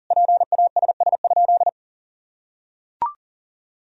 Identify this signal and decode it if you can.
Morse code, heard as a keyed tone, about 40 wpm, 700 Hz. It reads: PASS?
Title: Sentences from Top 400 Words and Abbreviations 40wpm